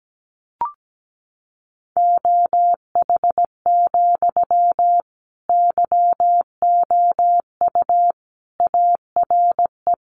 Morse code, heard as a keyed tone, about 17 wpm, 700 Hz. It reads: OH, YOU ARE
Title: Sentences from Top 500 Words 17wpm